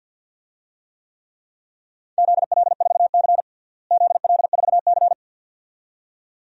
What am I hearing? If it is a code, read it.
Morse code, heard as a keyed tone, about 53 wpm, 700 Hz. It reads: ZL4C ZL4C
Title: DXCC-100 Call Signs - 4 Characters - Repeated 2x - 50wpm